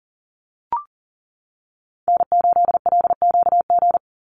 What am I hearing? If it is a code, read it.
Morse code, heard as a keyed tone, about 40 wpm, 700 Hz. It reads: N8LQG
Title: USA Call Signs 40wpm